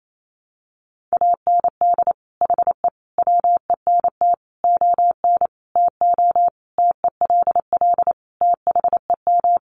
Morse code, heard as a keyed tone, about 28 wpm, 700 Hz. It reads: AND HE WENT ON TO TELL THEM
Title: Sentences from Top 900 Words 28wpm